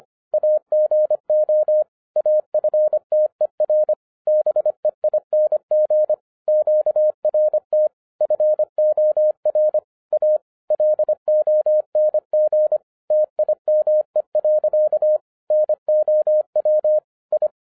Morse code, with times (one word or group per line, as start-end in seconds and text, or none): 0.3-1.8 AGO
2.2-3.9 AFTER
4.3-6.1 BEING
6.5-7.9 QRT
8.2-9.8 FOR
10.1-10.4 A
10.7-12.8 LONG
13.1-15.2 TIME.
15.5-17.0 NOW
17.3-17.5 I